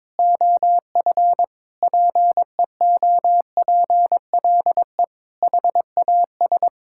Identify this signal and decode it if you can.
Morse code, heard as a keyed tone, about 21 wpm, 700 Hz. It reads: OF PEOPLE HAS
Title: Sentences from Top 1000 Words - Review Part 6 - 22wpm